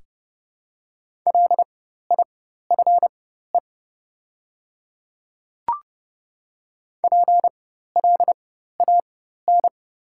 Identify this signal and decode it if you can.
Morse code, heard as a keyed tone, about 30 wpm, spaced as for 14 wpm, 700 Hz. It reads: LIFE PLAN
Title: 4x Character Spacing - Top 500 Words - 30wpm